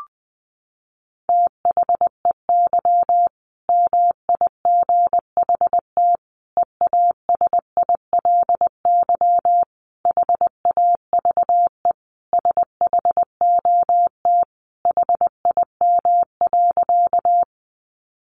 Morse code, heard as a keyed tone, about 20 wpm, 700 Hz. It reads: THEY MIGHT EASILY HAVE SHOT HIM.
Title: Sentences from Top 1100 Words 20wpm